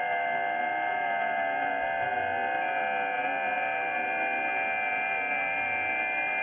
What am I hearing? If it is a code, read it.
FT8.
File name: IC-7100